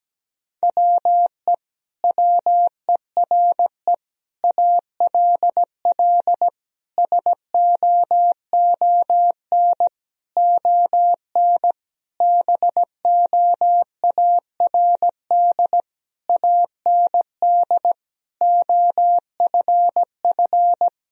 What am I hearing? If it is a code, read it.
Morse code, heard as a keyed tone, about 17 wpm, 700 Hz. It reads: WE WERE ALL SOON ON BOARD AND OFF
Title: Sentences from Top 1000 Words - Review Part 4 - 17wpm